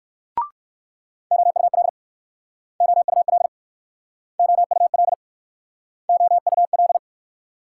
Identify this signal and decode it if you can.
Morse code, heard as a keyed tone, about 61 wpm, 700 Hz. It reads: OUL OUL OUL OUL